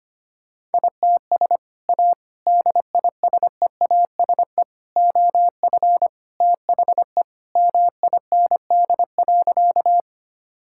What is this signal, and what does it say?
Morse code, heard as a keyed tone, about 25 wpm, 700 Hz. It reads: ITS A DISEASE OF THE MIND.
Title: Sentences from Top 1000 Words - Review Part 6 - 25wpm